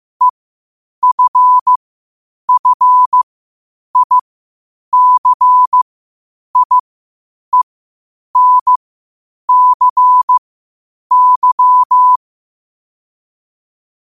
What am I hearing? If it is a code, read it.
Morse code, heard as a keyed tone, about 15 wpm, spaced as for 8 wpm, 1000 Hz. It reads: EFFICIENCY
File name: qotdfiles / 8wpm